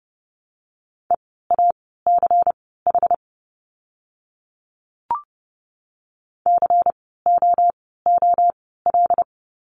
Morse code, heard as a keyed tone, about 30 wpm, spaced as for 17 wpm, 700 Hz. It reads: EACH COOL
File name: Lesson 16f - Letter c - Words 30wpm